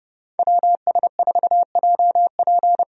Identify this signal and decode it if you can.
Morse code, heard as a keyed tone, about 30 wpm, 700 Hz. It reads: WS4JP
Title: USA Call Signs - 5 Characters - 30wpm